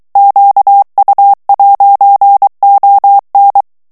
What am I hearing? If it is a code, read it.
Morse code, heard as a keyed tone, about 23 wpm, 800 Hz. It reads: QU'ON
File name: FRANCAIS TEXTE CLAIR 2 VITESSE 22 MOTS MINUTES